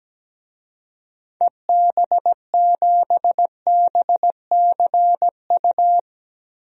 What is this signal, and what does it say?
Morse code, heard as a keyed tone, about 17 wpm, 700 Hz. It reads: EB7BCU
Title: DXCC-100 Call Signs - 6 Characters - Repeated 3x - 17wpm